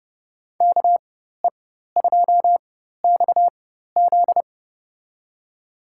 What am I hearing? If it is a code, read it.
Morse code, heard as a keyed tone, about 30 wpm, spaced as for 14 wpm, 700 Hz. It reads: KE2XZ
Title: USA Call Signs - 5 Characters - 4x Character Spacing - 30wpm